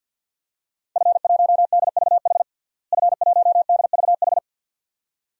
Morse code, heard as a keyed tone, about 53 wpm, 700 Hz. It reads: F1DVH F1DVH
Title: DXCC-100 Call Signs - 5 Characters - Repeated 2x - 50wpm